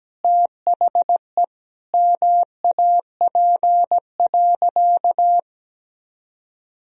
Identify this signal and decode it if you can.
Morse code, heard as a keyed tone, about 17 wpm, 700 Hz. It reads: THE MAP.
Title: Sentences from Top 1000 Words - Review Part 2 - 17wpm